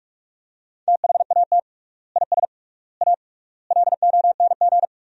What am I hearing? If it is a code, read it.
Morse code, heard as a keyed tone, about 46 wpm, 700 Hz. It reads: THAT IS A LONG
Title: 3x Word-Spacing - Sentences from Top 100 Words - 45wpm